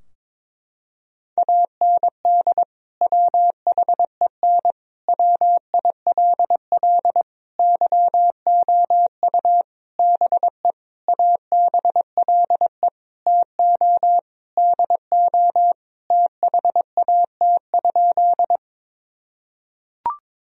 Morse code, heard as a keyed tone, about 21 wpm, 700 Hz. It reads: AND WHEN WILL YOU BE ABLE TO DO THAT?
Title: Sentences from Top 1000 Words - Review Part 2 - 22wpm